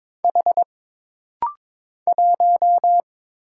Morse code, heard as a keyed tone, about 22 wpm, 700 Hz. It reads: H 1
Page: This rapid-fire timing is geared to push your ICR (Instant Character Recognition) skills.